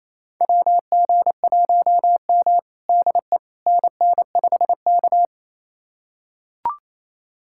Morse code, heard as a keyed tone, about 28 wpm, 700 Hz. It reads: WG1M DE NN5K
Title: USA Callsign DE Callsign - 4 Characters - 28wpm